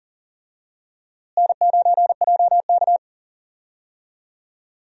Morse code, heard as a keyed tone, about 40 wpm, 700 Hz. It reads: N9JK
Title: USA Call Signs - 4 Characters - Repeated 2x - 40wpm